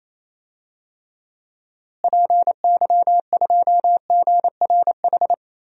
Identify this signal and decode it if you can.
Morse code, heard as a keyed tone, about 28 wpm, 700 Hz. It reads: PY2GRH
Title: DXCC-100 Call Signs - 6 Characters - Repeated 2x - 28wpm